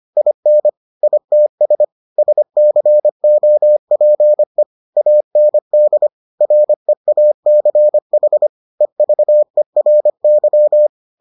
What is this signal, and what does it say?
Morse code, heard as a keyed tone, about 25 wpm, 600 Hz. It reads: IN ITS SCOPE AND REACH EVERY